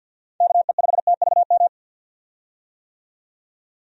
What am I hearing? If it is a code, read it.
Morse code, heard as a keyed tone, about 47 wpm, 700 Hz. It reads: KE5TVM
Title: USA Call Signs - 6 Characters - 50wpm